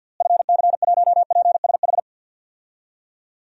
Morse code, heard as a keyed tone, about 49 wpm, 700 Hz. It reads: FC1PSH
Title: DXCC-100 Call Signs - 6 Characters - Repeated 2x - 50wpm